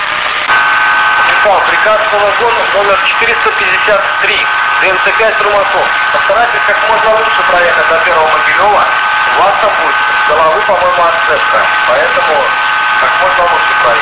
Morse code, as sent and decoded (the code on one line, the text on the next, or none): none